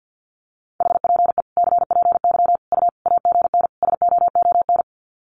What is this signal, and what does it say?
Morse code, heard as a keyed tone, about 50 wpm, 700 Hz. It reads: HPE CPY U AGN SOON
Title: Common QSO Phrases 50wpm